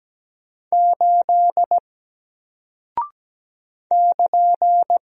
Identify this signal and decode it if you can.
Morse code, heard as a keyed tone, about 17 wpm, 700 Hz. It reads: 8 <KN>
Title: Single Character (Letter:Number:Punctuation:Prosign) Alt 17wpm